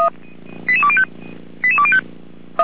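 Five-tone selective call (ZVEI), heard as quick runs of five tones, tones of about 70 ms. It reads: 88205 88205
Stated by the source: Five-tone ZVEI selective call received by SDR.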